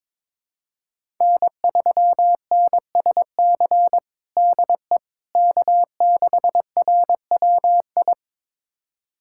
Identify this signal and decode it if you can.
Morse code, heard as a keyed tone, about 22 wpm, 700 Hz. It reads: N3NSC DE K6RWI
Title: USA Callsign DE Callsign - 5 Characters - 22wpm